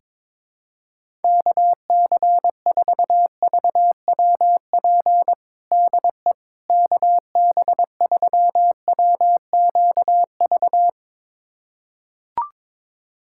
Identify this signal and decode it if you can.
Morse code, heard as a keyed tone, about 22 wpm, 700 Hz. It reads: KC4VWP DE KB3WQV
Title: USA Callsign DE Callsign 22wpm